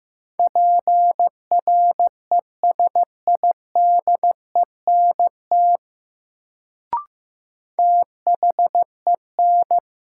Morse code, heard as a keyed tone, about 15 wpm, 700 Hz. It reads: PRESIDENT THEN
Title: Sets of 2 Words - New Word Drill - 15wpm